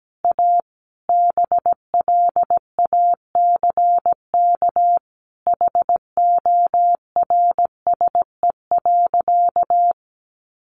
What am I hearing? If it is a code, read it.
Morse code, heard as a keyed tone, about 17 wpm, 700 Hz. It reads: A BLACK HORSE.
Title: Sentences from Top 1500 Words 17wpm